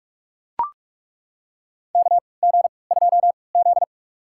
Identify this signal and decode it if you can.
Morse code, heard as a keyed tone, about 43 wpm, spaced as for 25 wpm, 700 Hz. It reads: KG2Z
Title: USA Call Signs - 4 Characters - 3x Character Spacing - 45wpm